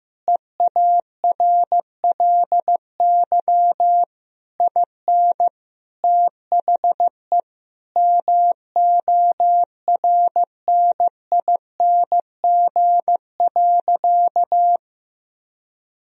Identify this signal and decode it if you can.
Morse code, heard as a keyed tone, about 15 wpm, 700 Hz. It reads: EARLY IN THE MORNING.
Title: Sentences from Top 1000 Words - Review Part 2 - 15wpm